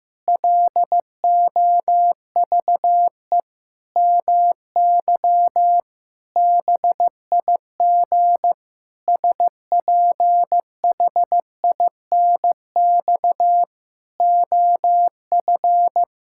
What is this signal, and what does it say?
Morse code, heard as a keyed tone, about 16 wpm, 700 Hz. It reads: LOVE MY BIG SPHINX OF
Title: Pangram Challenge v2 - 15wpm